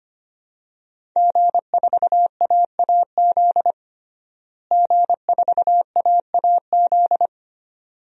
Morse code, heard as a keyed tone, about 25 wpm, 700 Hz. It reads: G4AAZ G4AAZ
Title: DXCC-100 Call Signs - 5 Characters - Repeated 2x - 25wpm